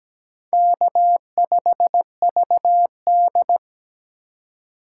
Morse code, heard as a keyed tone, about 17 wpm, 700 Hz. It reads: K5VD
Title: USA Call Signs - 3x Character Spacing - 17wpm